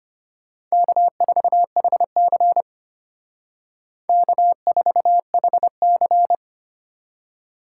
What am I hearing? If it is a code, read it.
Morse code, heard as a keyed tone, about 32 wpm, 700 Hz. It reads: K4HC K4HC